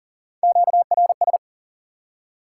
Morse code, heard as a keyed tone, about 53 wpm, 700 Hz. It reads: QRS